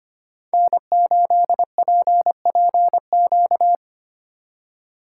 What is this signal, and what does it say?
Morse code, heard as a keyed tone, about 28 wpm, 700 Hz. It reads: N8PPQ
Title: Speed Racing - USA 1x3 Call Signs 25wpm